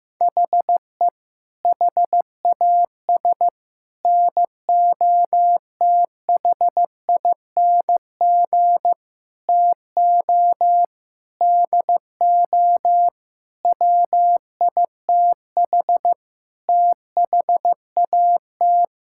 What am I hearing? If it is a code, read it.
Morse code, heard as a keyed tone, about 16 wpm, 700 Hz. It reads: HE HAS NOTHING TO DO WITH THAT
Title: Sentences from Top 1000 Words - Review Part 5 - 15wpm